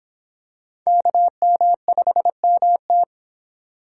Morse code, heard as a keyed tone, about 26 wpm, 700 Hz. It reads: KM5MT